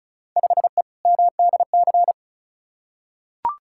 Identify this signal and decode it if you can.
Morse code, heard as a keyed tone, about 36 wpm, 700 Hz. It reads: 5E MDC
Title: Field Day Exchanges - 35wpm